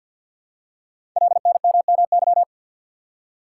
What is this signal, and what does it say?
Morse code, heard as a keyed tone, about 53 wpm, 700 Hz. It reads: LNMMY
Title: Code Groups - 5 Letters - 50wpm